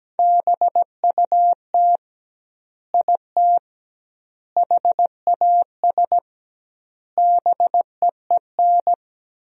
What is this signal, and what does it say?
Morse code, heard as a keyed tone, about 17 wpm, 700 Hz. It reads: BUT IT HAS BEEN
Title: 2x Word-Spacing - Sentences from Top 100 Words - 17wpm